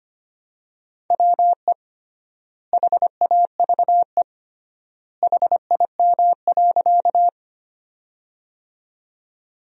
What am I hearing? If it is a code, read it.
Morse code, heard as a keyed tone, about 25 wpm, 700 Hz. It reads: WE HAVE HIM.
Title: 3x Word-Spacing - Sentences from Top 100 Words - 25wpm